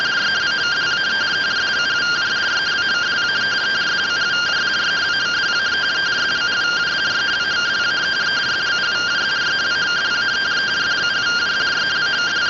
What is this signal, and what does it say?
Two-tone teleprinter signal, about 170 Hz shift.
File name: findings / CT 1 Scout 0001.gen-45 rtty rx